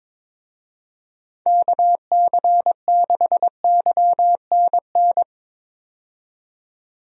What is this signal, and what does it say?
Morse code, heard as a keyed tone, about 22 wpm, 700 Hz. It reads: KC6YNN
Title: USA Call Signs - 6 Characters - 22wpm